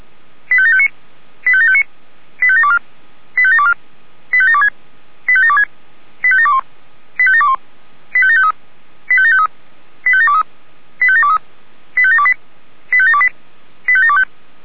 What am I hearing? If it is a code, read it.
Five-tone selective call (ZVEI), heard as quick runs of five tones, tones of about 80 ms. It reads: 85759 85759 85724 85724 85726 85726 85721 85721 85753 85753 85723 85723 85728 85728 85725
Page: /464/SDR/ZVEI/